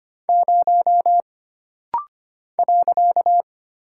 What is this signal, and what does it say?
Morse code, heard as a keyed tone, about 25 wpm, 700 Hz. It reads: 0 .
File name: Single Character (Letter:Number:Punctuation:Prosign) - Rapid-Fire - 25wpm